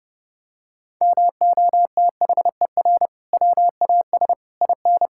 Morse code, heard as a keyed tone, about 30 wpm, 700 Hz. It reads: MOTHER WAS IN
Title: Sentences from Top 1000 Words - Review Part 6 - 30wpm